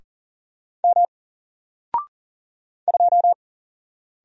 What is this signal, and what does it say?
Morse code, heard as a keyed tone, about 40 wpm, 700 Hz. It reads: M 2
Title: Single Letter-Number Alt 40wpm